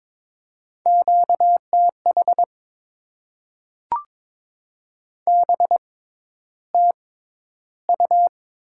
Morse code, heard as a keyed tone, about 21 wpm, spaced as for 11 wpm, 700 Hz. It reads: QTH BTU
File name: Top 90 QSO Elements - 6x Character Spacing - 22wpm